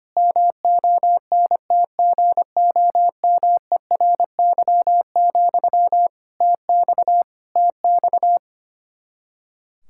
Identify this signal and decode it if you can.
Morse code, heard as a keyed tone, about 25 wpm, 700 Hz. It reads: MONTGOMERY, TX TX
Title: Practice QSO - Exchange 1 - Beginning - 25wpm